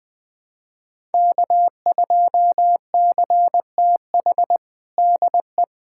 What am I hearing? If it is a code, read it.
Morse code, heard as a keyed tone, about 20 wpm, 700 Hz. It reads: K2CTH DE
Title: USA Callsign DE Callsign - 5 Characters - 20wpm